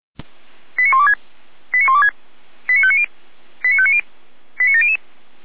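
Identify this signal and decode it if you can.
Five-tone selective call (ZVEI), heard as quick runs of five tones, tones of about 70 ms. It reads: 89126 89126 89590 89590 89700
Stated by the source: ZVEI five-tone selective call received on SDR.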